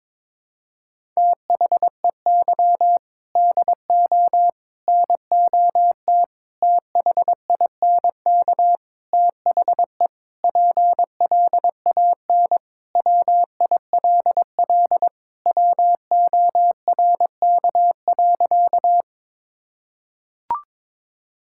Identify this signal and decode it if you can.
Morse code, heard as a keyed tone, about 22 wpm, 700 Hz. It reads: THEY DO NOT THINK THE PLAN WILL WORK.
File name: practice / 22wpm / Sentences with 4-Word Phrases - 22wpm